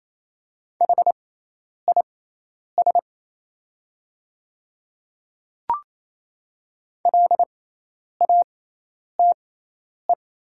Morse code, heard as a keyed tone, about 28 wpm, spaced as for 10 wpm, 700 Hz. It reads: HIS LATE
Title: 6x Character Spacing - Top 300 Words - 28wpm